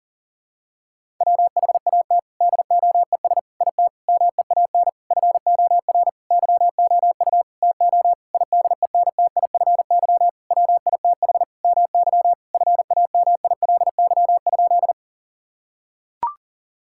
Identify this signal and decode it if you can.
Morse code, heard as a keyed tone, about 40 wpm, 700 Hz. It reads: WHAT DOES IT MEAN FOR YOU TO IDENTIFY WITH MY FAMILY?
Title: Sentences from Top 1000 Words - Review Part 6 - 40wpm